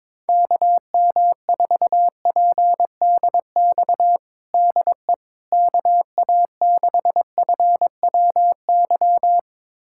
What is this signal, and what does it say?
Morse code, heard as a keyed tone, about 22 wpm, 700 Hz. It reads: KM4PDX DE KA6FWY
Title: USA Callsign DE Callsign - 6 Characters - 22wpm